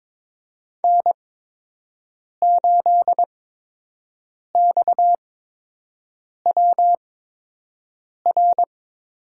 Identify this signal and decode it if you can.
Morse code, heard as a keyed tone, about 22 wpm, spaced as for 6 wpm, 700 Hz. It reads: N8XWR
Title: USA Call Signs - 5 Characters - 8x Character Spacing - 22wpm